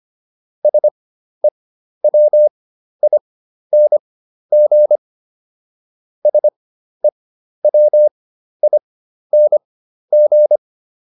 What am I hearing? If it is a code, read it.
Morse code, heard as a keyed tone, about 25 wpm, spaced as for 12 wpm, 600 Hz. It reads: SEWING SEWING